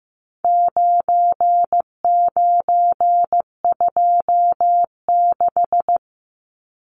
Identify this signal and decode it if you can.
Morse code, heard as a keyed tone, about 15 wpm, 700 Hz. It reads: 9926
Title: Numbers - 4-Digit 15wpm